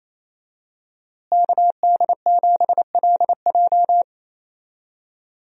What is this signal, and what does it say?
Morse code, heard as a keyed tone, about 27 wpm, 700 Hz. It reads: KD7LJ
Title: USA Call Signs - 5 Characters - Repeated 3x - 28wpm